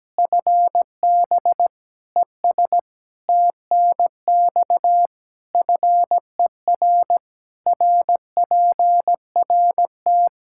Morse code, heard as a keyed tone, about 17 wpm, 700 Hz. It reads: FB ES TNX FER RPRT
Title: Practice QSO - Exchange 2 - Info - 17wpm